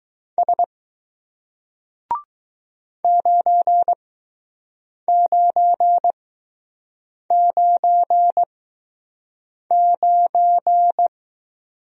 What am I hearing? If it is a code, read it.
Morse code, heard as a keyed tone, about 20 wpm, spaced as for 6 wpm, 700 Hz. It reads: S 9999